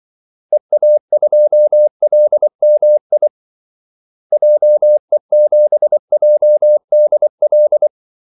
Morse code, heard as a keyed tone, about 24 wpm, 600 Hz. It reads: EA2LMI JE7JDL
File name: Callsigns-24wpm